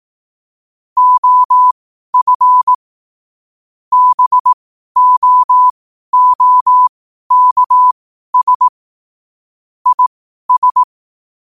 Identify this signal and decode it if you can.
Morse code, heard as a keyed tone, about 18 wpm, spaced as for 12 wpm, 1000 Hz. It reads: OF BOOKS IS
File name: qotdfiles / 12wpm